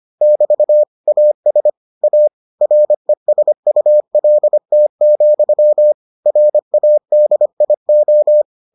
Morse code, heard as a keyed tone, about 25 wpm, 600 Hz. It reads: <BT> AS A RESULT, RADIO